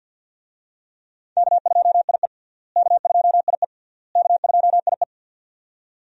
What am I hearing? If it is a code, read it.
Morse code, heard as a keyed tone, about 51 wpm, 700 Hz. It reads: K2IE K2IE K2IE
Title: USA Call Signs - 4 Characters - Repeated 3x - 50wpm